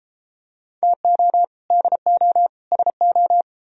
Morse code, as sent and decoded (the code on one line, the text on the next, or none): - --- / -.. --- / ... ---
TO DO SO